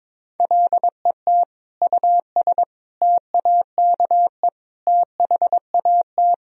Morse code, heard as a keyed tone, about 22 wpm, 700 Hz. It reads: LET US TAKE THAT
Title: Sentences from Top 1000 Words - Review Part 2 - 22wpm